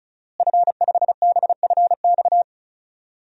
Morse code, heard as a keyed tone, about 36 wpm, 700 Hz. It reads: F5BFX
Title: DXCC-100 Call Signs - 5 Characters - Repeated 3x - 35wpm